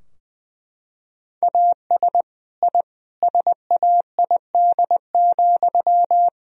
Morse code, heard as a keyed tone, about 19 wpm, 700 Hz. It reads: AS I SAID,
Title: Sentences from Top 1000 Words - Review Part 3 - 20wpm